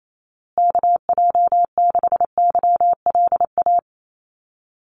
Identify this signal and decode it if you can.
Morse code, heard as a keyed tone, about 28 wpm, 700 Hz. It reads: KJ6YLA
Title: USA Call Signs Repeated 2x 28wpm